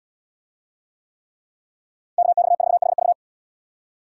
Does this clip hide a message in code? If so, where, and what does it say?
Yes: Morse code from 2.2 to 3.1 s, heard as a keyed tone, about 71 wpm, 700 Hz. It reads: KX4HV